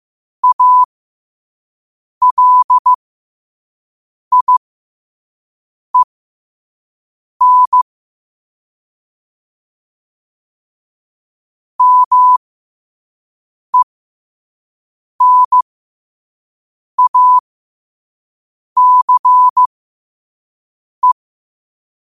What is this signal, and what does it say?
Morse code, heard as a keyed tone, about 15 wpm, spaced as for 5 wpm, 1000 Hz. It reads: ALIEN MENACE
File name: qotdfiles / 5wpm